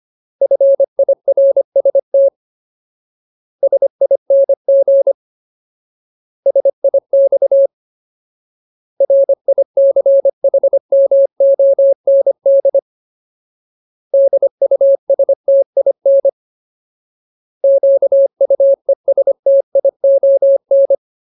Morse code, heard as a keyed tone, about 25 wpm, 550 Hz. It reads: FIRST SING SIX RICHMOND DUSTIN QUESTION
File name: Ultimate Word List / 25WPM Ultimate